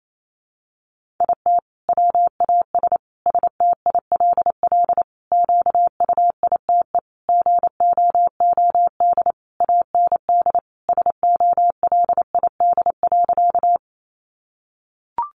Morse code, heard as a keyed tone, about 28 wpm, 700 Hz. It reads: IT WAS STILL QUITE GOOD AND SOLID.
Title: Sentences from Top 1500 Words 28wpm